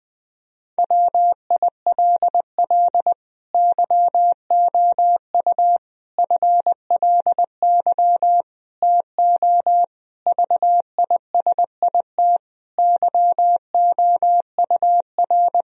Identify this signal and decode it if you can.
Morse code, heard as a keyed tone, about 20 wpm, 700 Hz. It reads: WILL YOU FLY TO VISIT YOUR
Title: Sentences from Top 1000 Words - Review Part 2 - 20wpm